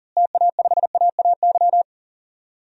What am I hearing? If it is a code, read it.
Morse code, heard as a keyed tone, about 40 wpm, 700 Hz. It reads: TA5AAY
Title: DXCC-100 Call Signs - 6 Characters - Repeated 2x - 40wpm